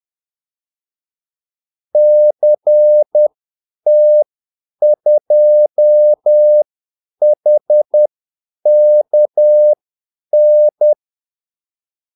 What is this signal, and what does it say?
Morse code, heard as a keyed tone, about 10 wpm, spaced as for 8 wpm, 600 Hz. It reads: CT2HKN
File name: Callsigns-10-8wpm